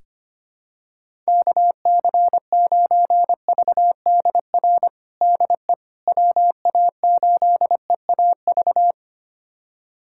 Morse code, heard as a keyed tone, about 25 wpm, 700 Hz. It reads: KC9VDR DE WA8EAV
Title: USA Callsign DE Callsign 25wpm